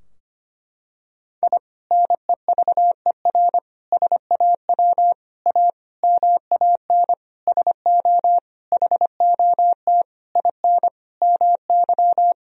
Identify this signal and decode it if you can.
Morse code, heard as a keyed tone, about 25 wpm, 700 Hz. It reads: I NEVER SAW A MAN SO HOT IN MY
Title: Sentences from Top 1000 Words - Review Part 3 - 25wpm